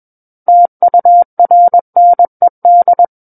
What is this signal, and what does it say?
Morse code, heard as a keyed tone, about 21 wpm, 700 Hz. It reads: TURNED
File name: Book One / 12-Chapter 12 - 21 WPM